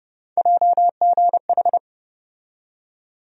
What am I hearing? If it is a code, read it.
Morse code, heard as a keyed tone, about 30 wpm, 700 Hz. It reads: JGH
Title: USA Call Sign Postfixes - 3 Letters 30wpm